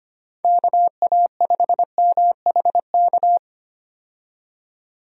Morse code, heard as a keyed tone, about 25 wpm, 700 Hz. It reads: KA5MHK
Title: USA Call Signs - 6 Characters - 25wpm